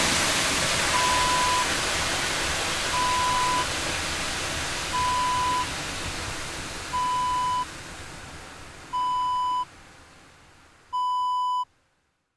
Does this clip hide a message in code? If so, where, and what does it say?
No.